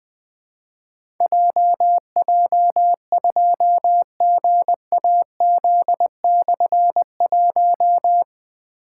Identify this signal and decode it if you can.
Morse code, heard as a keyed tone, about 20 wpm, 700 Hz. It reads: JJ2GAZ/1
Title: Difficult Call Signs 20wpm